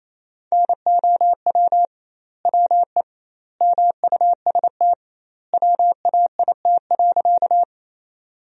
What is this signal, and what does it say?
Morse code, heard as a keyed tone, about 28 wpm, 700 Hz. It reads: NOW WE MUST WAIT.
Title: 2x Word-Spacing - Sentences from Top 500 Words - 28wpm